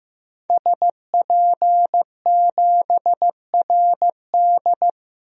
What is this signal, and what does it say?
Morse code, heard as a keyed tone, about 15 wpm, 700 Hz. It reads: SP7RD
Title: DXCC-100 Call Signs - 5 Characters - Repeated 2x - 15wpm